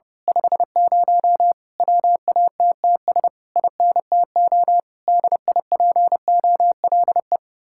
Morse code, heard as a keyed tone, about 28 wpm, 700 Hz. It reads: 50 WATTS INTO DIPOLE
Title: Speed-Racing Common QSO Phrases - 30wpm